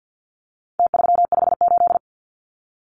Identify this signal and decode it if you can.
Morse code, heard as a keyed tone, about 49 wpm, 700 Hz. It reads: T358
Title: Numbers - Contest Style - 4-Digit 50wpm